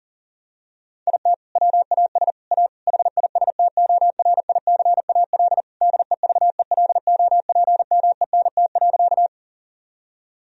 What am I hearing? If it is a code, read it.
Morse code, heard as a keyed tone, about 40 wpm, 700 Hz. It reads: IT WAS A HISTORICAL DEVELOPMENT.
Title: Sentences from Top 1000 Words - Review Part 4 - 40wpm